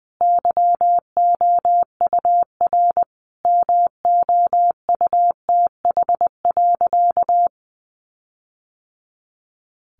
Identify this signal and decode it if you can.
Morse code, heard as a keyed tone, about 20 wpm, 700 Hz. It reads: YOUR MOUTH.
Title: Sentences from Top 1500 Words 20wpm